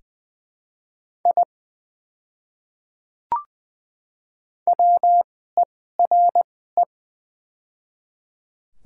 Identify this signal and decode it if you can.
Morse code, heard as a keyed tone, about 20 wpm, spaced as for 14 wpm, 700 Hz. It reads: I WERE
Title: 2x Character Spacing - Top 200 Words - 20wpm